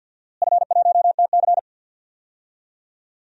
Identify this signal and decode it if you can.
Morse code, heard as a keyed tone, about 52 wpm, 700 Hz. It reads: F1TC
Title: DXCC-100 Call Signs - 4 Characters - Repeated 2x - 50wpm